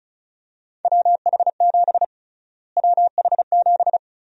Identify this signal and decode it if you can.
Morse code, heard as a keyed tone, about 36 wpm, 700 Hz. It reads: WH7 WH7
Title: USA Call Sign Prefixes - Repeated 2x - 35wpm